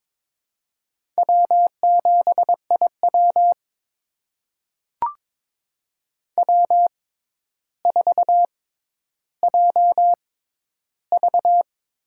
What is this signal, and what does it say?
Morse code, heard as a keyed tone, about 22 wpm, spaced as for 11 wpm, 700 Hz. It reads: W7IW W4JV
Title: USA Call Signs - 4 Characters - 6x Character Spacing - 22wpm